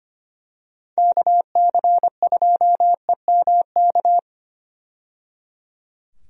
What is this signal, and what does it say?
Morse code, heard as a keyed tone, about 25 wpm, 700 Hz. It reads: KC2EMK
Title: USA Call Signs - 6 Characters - 25wpm